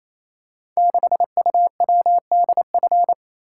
Morse code, heard as a keyed tone, about 28 wpm, 700 Hz. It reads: 6UWDF
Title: Code Groups - 5 Letter:Numbers - 28wpm